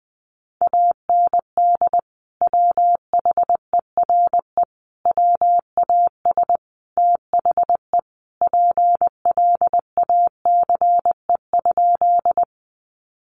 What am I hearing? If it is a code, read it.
Morse code, heard as a keyed tone, about 20 wpm, 700 Hz. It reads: AND WHERE WAS THE PLACE?
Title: Sentences from Top 200 Words 20wpm